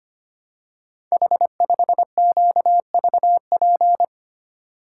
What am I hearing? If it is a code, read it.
Morse code, heard as a keyed tone, about 25 wpm, 700 Hz. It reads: H5QVP
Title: Code Groups - 5 Letter:Numbers - 25wpm